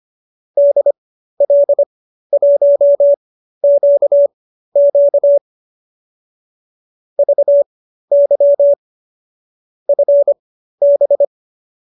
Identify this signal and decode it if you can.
Morse code, heard as a keyed tone, about 25 wpm, spaced as for 13 wpm, 590 Hz. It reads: DL1QQ VY FB